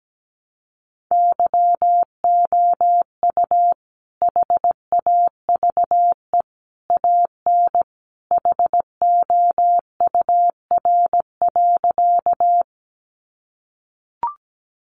Morse code, heard as a keyed tone, about 17 wpm, 700 Hz. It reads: YOU HAVE AN HOUR.
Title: Sentences from Top 400 Words 17wpm